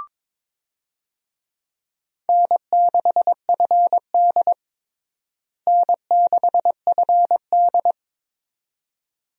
Morse code, heard as a keyed tone, about 22 wpm, 700 Hz. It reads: N6FD N6FD
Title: USA Call Signs - 4 Characters - Repeated 2x - 22wpm